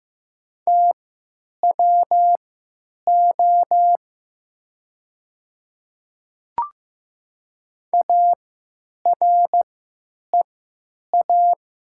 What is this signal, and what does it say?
Morse code, heard as a keyed tone, about 15 wpm, spaced as for 9 wpm, 700 Hz. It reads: TWO AREA
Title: 3x Character Spacing - Top 500 Words - 15wpm